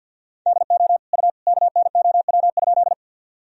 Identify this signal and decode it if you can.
Morse code, heard as a keyed tone, about 51 wpm, 700 Hz. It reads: DO U KNOW?
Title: Sentences from Top 100 Words and Abbreviations 50wpm